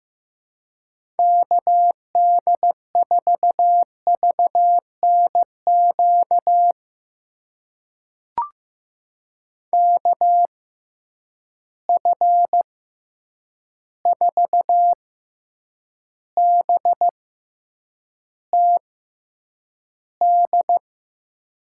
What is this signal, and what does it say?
Morse code, heard as a keyed tone, about 15 wpm, spaced as for 8 wpm, 700 Hz. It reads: KD4VNQ KF4BTD
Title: USA Call Signs - 6 Characters - 6x Character Spacing - 15wpm